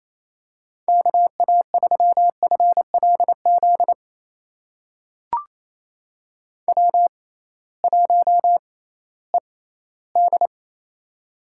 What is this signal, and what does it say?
Morse code, heard as a keyed tone, about 28 wpm, spaced as for 16 wpm, 700 Hz. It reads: KA3FLZ W1ED
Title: USA Call Signs - 6x Character Spacing - 28wpm